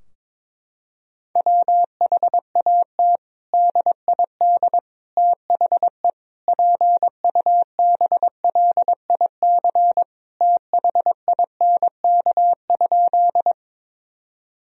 Morse code, heard as a keyed tone, about 21 wpm, 700 Hz. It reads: WHAT DID THE PUBLIC THINK?
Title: Sentences from Top 1000 Words - Review Part 3 - 22wpm